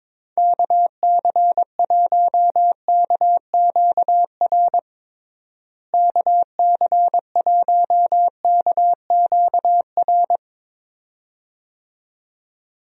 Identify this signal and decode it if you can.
Morse code, heard as a keyed tone, about 21 wpm, 700 Hz. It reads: KC1KQR KC1KQR
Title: USA Call Signs - 6 Characters - Repeated 2x - 22wpm